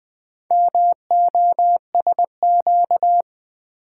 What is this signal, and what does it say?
Morse code, heard as a keyed tone, about 21 wpm, 700 Hz. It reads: MOSQ
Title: Build A Word - 2 Letters At A Time - 8-Letter Words - 20wpm